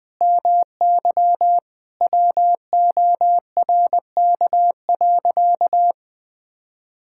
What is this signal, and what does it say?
Morse code, heard as a keyed tone, about 20 wpm, 700 Hz. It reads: MY WORK.
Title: Sentences with 7-Word Phrases - 20wpm